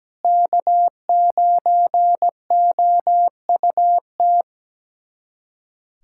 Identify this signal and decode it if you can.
Morse code, heard as a keyed tone, about 17 wpm, 700 Hz. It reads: K9OUT
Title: USA Callsign DE Callsign - 5 Characters - 17wpm